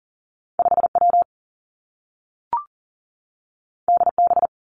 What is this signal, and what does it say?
Morse code, heard as a keyed tone, about 39 wpm, 700 Hz. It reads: 5 W D B
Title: Two Challenging Letter-Number Combinations 40wpm